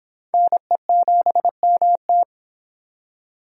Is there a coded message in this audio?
Yes: Morse code from 0.3 to 2.2 s, heard as a keyed tone, about 25 wpm, 700 Hz. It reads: NE7MT